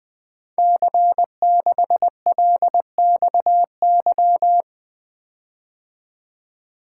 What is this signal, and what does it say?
Morse code, heard as a keyed tone, about 20 wpm, 700 Hz. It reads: C6LXY
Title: Code Groups - 5 Letter:Numbers - 20wpm